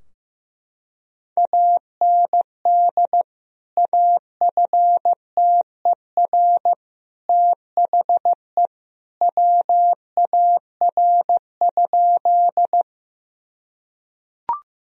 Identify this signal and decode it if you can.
Morse code, heard as a keyed tone, about 16 wpm, 700 Hz. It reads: AND AFTER THE WAR?
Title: Sentences from Top 1000 Words - Review Part 2 - 15wpm